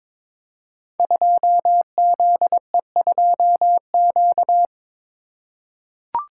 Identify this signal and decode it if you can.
Morse code, heard as a keyed tone, about 22 wpm, 700 Hz. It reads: 2ZE2Q
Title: Code Groups - 5 Letter:Numbers - 22wpm